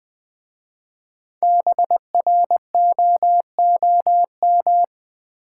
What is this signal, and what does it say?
Morse code, heard as a keyed tone, about 20 wpm, 700 Hz. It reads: BROOM
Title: Easy Words with Hints - 20wpm